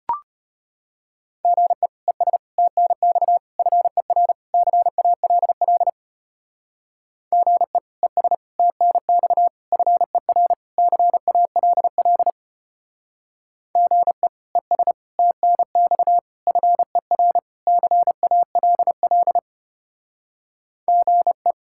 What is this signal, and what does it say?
Morse code, heard as a keyed tone, about 30 wpm, 700 Hz. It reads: GE ES TNX FER CALL GE ES TNX FER CALL GE ES TNX FER CALL GE
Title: Speed-Racing Common QSO Phrases - 25wpm